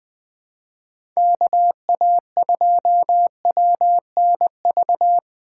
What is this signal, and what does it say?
Morse code, heard as a keyed tone, about 20 wpm, 700 Hz. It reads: KA2WNV
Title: USA Call Signs - 6 Characters - 4x Character Spacing - 20wpm